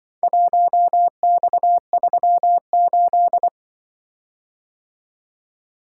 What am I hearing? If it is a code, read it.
Morse code, heard as a keyed tone, about 25 wpm, 700 Hz. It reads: 1X38
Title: Speed-Racing - Four Letter-Number Combinations - 20wpm